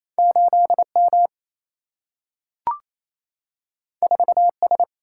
Morse code, heard as a keyed tone, about 28 wpm, 700 Hz. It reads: 8 M 4 S
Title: Single-Letter Single-Number 28wpm